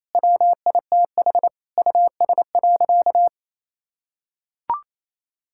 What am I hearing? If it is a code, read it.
Morse code, heard as a keyed tone, about 27 wpm, 700 Hz. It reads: WITH US.
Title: Sentences from Top 1000 Words - Review Part 6 - 28wpm